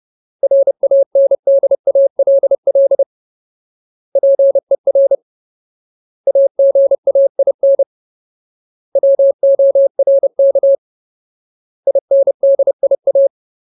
Morse code, heard as a keyed tone, about 30 wpm, 550 Hz. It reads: RANDALL PER AGAIN WORK INDIA
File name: Ultimate Word List / 30WPM Ultimate